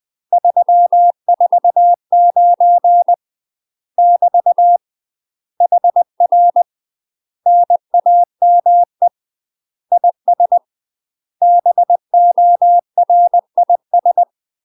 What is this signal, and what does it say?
Morse code, heard as a keyed tone, about 20 wpm, 700 Hz. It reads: 349 <BT> HR NAME IS BORIS
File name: UK MASTER QSOs TRAINING / 20WPM / UK MASTER QSOs 0040